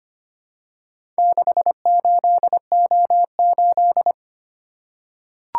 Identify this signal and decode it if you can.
Morse code, heard as a keyed tone, about 25 wpm, 700 Hz. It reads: 68O8
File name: practice / 25wpm / Four Letter-Number Combinations - 25wpm